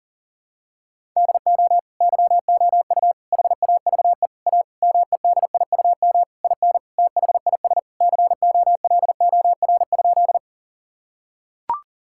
Morse code, heard as a keyed tone, about 40 wpm, 700 Hz. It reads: DO YOU HAVE A MEDIUM IN THIS COLOR?
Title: Sentences from Top 1000 Words - Review Part 6 - 40wpm